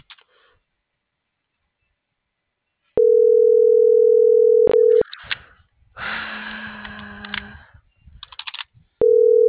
Ringback tone (North America), 440 + 480 Hz.